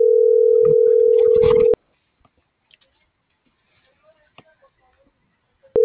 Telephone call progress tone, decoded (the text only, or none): ringback tone